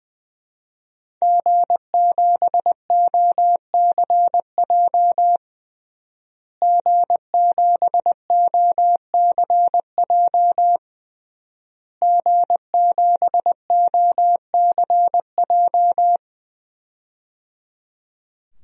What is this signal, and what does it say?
Morse code, heard as a keyed tone, about 20 wpm, 700 Hz. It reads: G7OCJ G7OCJ G7OCJ
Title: DXCC-100 Call Signs - 5 Characters - Repeated 3x - 20wpm